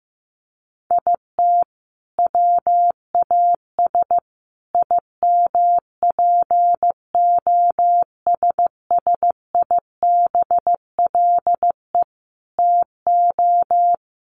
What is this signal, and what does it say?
Morse code, heard as a keyed tone, about 15 wpm, 700 Hz. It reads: IT WAS IMPOSSIBLE TO
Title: Sentences from Top 1500 Words 15wpm